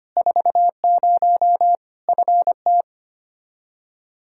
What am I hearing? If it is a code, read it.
Morse code, heard as a keyed tone, about 25 wpm, 700 Hz. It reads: 40 FT
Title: Practice QSO - Exchange 2 - Info - 25wpm